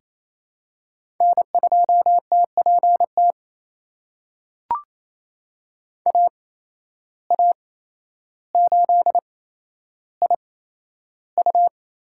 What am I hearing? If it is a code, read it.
Morse code, heard as a keyed tone, about 27 wpm, spaced as for 12 wpm, 700 Hz. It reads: N2TPT AA8IU
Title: USA Call Signs - 5 Characters - 8x Character Spacing - 28wpm